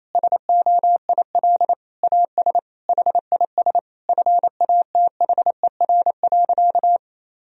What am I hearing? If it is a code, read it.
Morse code, heard as a keyed tone, about 28 wpm, 700 Hz. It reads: SOIL AS HIS FATHER.
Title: Sentences from Top 1000 Words - Review Part 4 - 28wpm